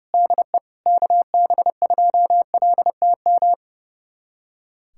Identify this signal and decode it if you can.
Morse code, heard as a keyed tone, about 30 wpm, 700 Hz. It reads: DE KB2LTM
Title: USA Callsign DE Callsign 30wpm